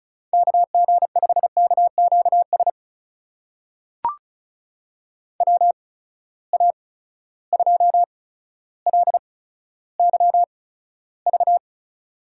Morse code, heard as a keyed tone, about 36 wpm, spaced as for 15 wpm, 700 Hz. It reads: KG5KQS WA2LYV
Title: USA Call Signs - 6 Characters - 8x Character Spacing - 35wpm